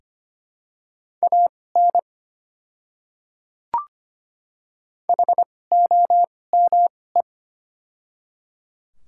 Morse code, heard as a keyed tone, about 25 wpm, spaced as for 18 wpm, 700 Hz. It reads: AN HOME
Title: 2x Character Spacing - Top 300 Words - 25wpm